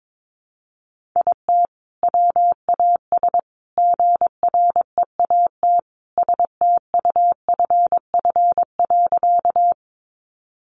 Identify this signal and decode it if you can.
Morse code, heard as a keyed tone, about 22 wpm, 700 Hz. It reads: IT WAS GREAT STUFF.
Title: Sentences from Top 1400 Words 22wpm